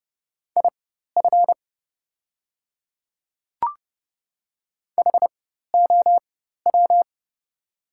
Morse code, heard as a keyed tone, about 30 wpm, spaced as for 14 wpm, 700 Hz. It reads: IF HOW
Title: 4x Character Spacing - Top 100 Words - 30wpm